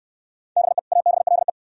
Morse code, heard as a keyed tone, about 60 wpm, 700 Hz. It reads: BE ABLE